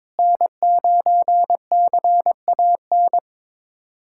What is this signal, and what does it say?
Morse code, heard as a keyed tone, about 22 wpm, 700 Hz. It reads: N9CAN
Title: USA Callsign DE Callsign - 5 Characters - 22wpm